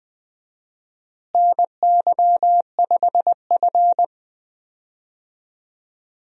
Morse code, heard as a keyed tone, about 20 wpm, 700 Hz. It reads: NY5F
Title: USA Call Signs - 4 Characters - Repeated 3x - 20wpm